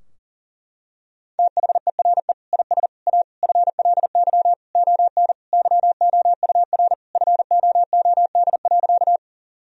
Morse code, heard as a keyed tone, about 37 wpm, 700 Hz. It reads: THERE IS A FLY ON YOUR FOOD.
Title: Sentences from Top 1000 Words - Review Part 2 - 40wpm